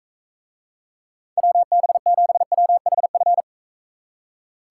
Morse code, heard as a keyed tone, about 41 wpm, 700 Hz. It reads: WB7WHF